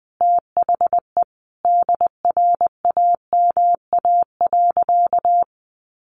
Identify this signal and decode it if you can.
Morse code, heard as a keyed tone, about 20 wpm, 700 Hz. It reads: THE DRAMA.
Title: Sentences from Top 1500 Words 20wpm